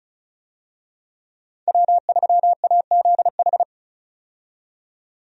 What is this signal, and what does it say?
Morse code, heard as a keyed tone, about 36 wpm, 700 Hz. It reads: W3AZH
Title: USA Call Signs - 5 Characters - 35wpm